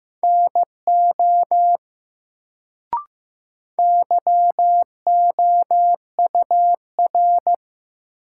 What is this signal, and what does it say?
Morse code, heard as a keyed tone, about 15 wpm, 700 Hz. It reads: NO YOUR
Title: Sentences from Top 100 Words - New Word Drill - 15wpm